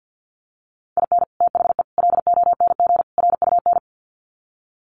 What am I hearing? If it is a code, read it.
Morse code, heard as a keyed tone, about 50 wpm, 700 Hz. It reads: IN THE LONG RUN
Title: Sets of 4 Words 50wpm